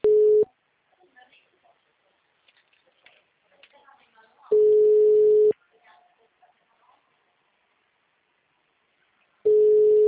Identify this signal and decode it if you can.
Ringback tone (Europe, 425 Hz), with speech.